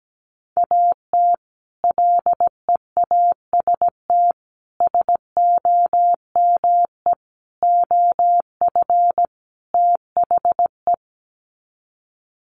Morse code, heard as a keyed tone, about 17 wpm, 700 Hz. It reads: AT LEAST SOME OF THE
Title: Sets of 5 Words Encore 17wpm